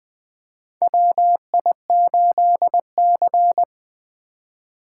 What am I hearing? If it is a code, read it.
Morse code, heard as a keyed tone, about 20 wpm, 700 Hz. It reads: WI8C
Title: USA Call Signs - 4 Characters - 2x Character Spacing - 20wpm